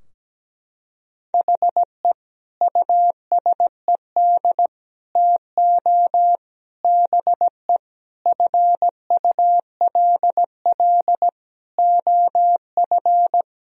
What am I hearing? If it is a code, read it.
Morse code, heard as a keyed tone, about 17 wpm, 700 Hz. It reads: HE USED TO BE FULL OF
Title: Sentences from Top 1000 Words - Review Part 4 - 17wpm